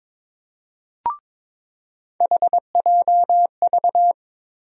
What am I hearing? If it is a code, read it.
Morse code, heard as a keyed tone, about 22 wpm, 700 Hz. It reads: HJV
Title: USA Call Sign Postfixes - 3 Letters 22wpm